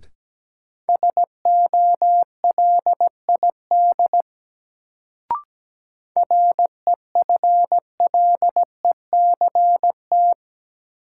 Morse code, heard as a keyed tone, about 17 wpm, 700 Hz. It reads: SOLID REFLECT
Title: Sentences from Top 1500 Words - New Word Drill - 17wpm